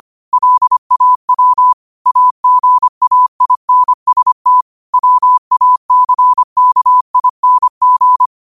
Morse code, heard as a keyed tone, about 25 wpm, 1000 Hz. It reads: LAW AGAINST WACKING
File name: qotdfiles / 25wpm